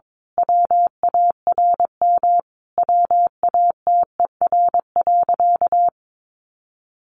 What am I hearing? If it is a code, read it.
Morse code, heard as a keyed tone, about 22 wpm, 700 Hz. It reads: WARM WATER.
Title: A Sound Of Thunder 22wpm